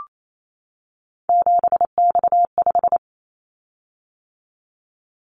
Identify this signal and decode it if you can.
Morse code, heard as a keyed tone, about 28 wpm, 700 Hz. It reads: 7X5
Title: Callsign Prefixes for Top 100 DXCC - 28wpm